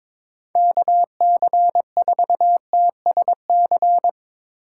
Morse code, heard as a keyed tone, about 23 wpm, 700 Hz. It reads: KC4TSC
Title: USA Call Signs - 6 Characters - 3x Character Spacing - 22wpm